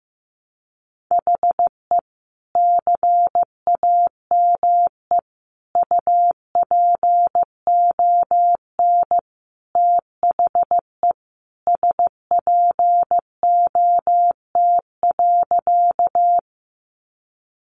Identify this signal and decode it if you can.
Morse code, heard as a keyed tone, about 15 wpm, 700 Hz. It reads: HE CAME UPON THE SPOT.
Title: Sentences from Top 1100 Words 15wpm